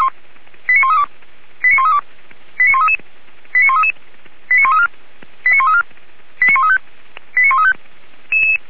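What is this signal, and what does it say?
Five-tone selective call (ZVEI), heard as quick runs of five tones, tones of about 70 ms. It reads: 89132 89132 89133 89134 89134 89135 89135
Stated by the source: ZVEI five-tone call received by SDR.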